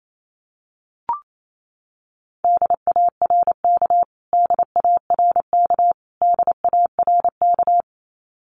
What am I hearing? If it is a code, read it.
Morse code, heard as a keyed tone, about 28 wpm, 700 Hz. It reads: DARK DARK DARK
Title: Top 500 Words Repeated 28wpm